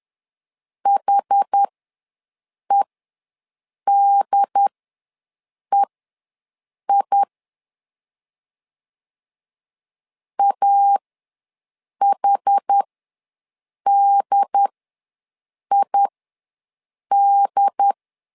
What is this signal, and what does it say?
Morse code, heard as a keyed tone, about 10 wpm, spaced as for 6 wpm, 800 Hz. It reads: HEDEI AHDID
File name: LEZIONI / 2 UHDN / 2 UHD ws 10wpm 2058